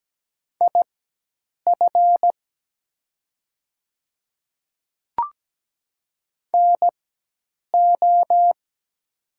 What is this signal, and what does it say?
Morse code, heard as a keyed tone, about 17 wpm, spaced as for 8 wpm, 700 Hz. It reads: IF NO
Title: 4x Character Spacing - Top 100 Words - 17wpm